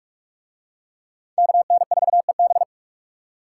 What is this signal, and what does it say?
Morse code, heard as a keyed tone, about 46 wpm, 700 Hz. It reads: KN4EB